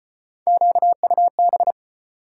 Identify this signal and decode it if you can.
Morse code, heard as a keyed tone, about 34 wpm, 700 Hz. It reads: QUB